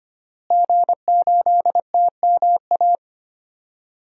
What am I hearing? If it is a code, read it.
Morse code, heard as a keyed tone, about 25 wpm, 700 Hz. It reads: G8TMA
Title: DXCC-100 Call Signs - 5 Characters - Repeated 2x - 25wpm